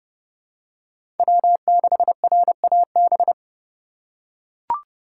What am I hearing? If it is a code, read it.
Morse code, heard as a keyed tone, about 29 wpm, 700 Hz. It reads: W6RAB